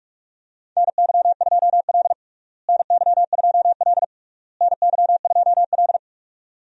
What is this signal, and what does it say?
Morse code, heard as a keyed tone, about 42 wpm, 700 Hz. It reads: NY2L NY2L NY2L
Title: USA Call Signs - 4 Characters - Repeated 3x - 45wpm